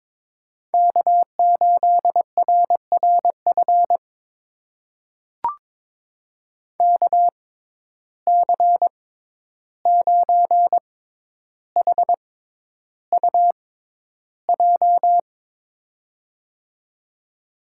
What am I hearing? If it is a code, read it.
Morse code, heard as a keyed tone, about 21 wpm, spaced as for 11 wpm, 700 Hz. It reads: K8RRF KC9HUJ
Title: USA Call Signs - 6x Character Spacing - 22wpm